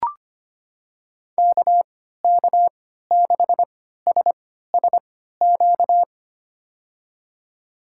Morse code, heard as a keyed tone, about 25 wpm, spaced as for 14 wpm, 700 Hz. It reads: KK6SSQ
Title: USA Call Signs - 6 Characters - 3x Character Spacing - 25wpm